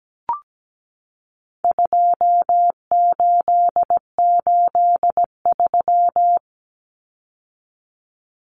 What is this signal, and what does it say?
Morse code, heard as a keyed tone, about 17 wpm, 700 Hz. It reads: 2883
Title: Common Cut Numbers - 4-Digit 17wpm